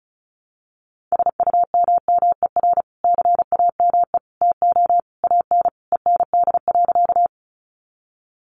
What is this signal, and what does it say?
Morse code, heard as a keyed tone, about 34 wpm, 700 Hz. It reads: SUMMER CAME TO AN END.
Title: Sentences from Top 1100 Words 35wpm